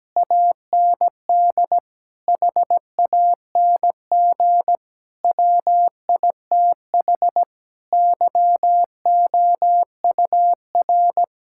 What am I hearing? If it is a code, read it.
Morse code, heard as a keyed tone, about 17 wpm, 700 Hz. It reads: AND HANG WITH YOUR
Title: Sentences from Top 1600 Words 17wpm